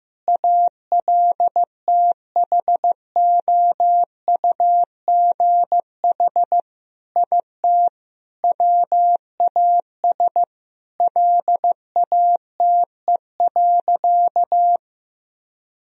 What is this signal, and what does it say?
Morse code, heard as a keyed tone, about 15 wpm, 700 Hz. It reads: ALTHOUGH IT WAS LATE.
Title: Sentences from Top 1000 Words - Review Part 3 - 15wpm